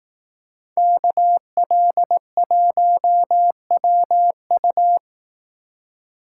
Morse code, heard as a keyed tone, about 18 wpm, 700 Hz. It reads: KL1WU
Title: Speed Racing - USA 2x2 Call Signs 15wpm